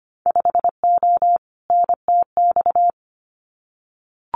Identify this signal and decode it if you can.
Morse code, heard as a keyed tone, about 25 wpm, 690 Hz. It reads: 5O NTX
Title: Winter Field Day Exchanges Encore 25wpm